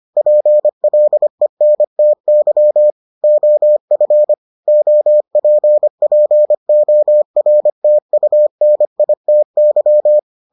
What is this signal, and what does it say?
Morse code, heard as a keyed tone, about 25 wpm, 600 Hz. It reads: PLENTY OF OPPORTUNITY